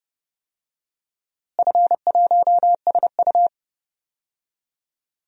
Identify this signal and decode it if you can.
Morse code, heard as a keyed tone, about 30 wpm, 700 Hz. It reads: F1SU
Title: DXCC-100 Call Signs - 4 Characters - Repeated 2x - 30wpm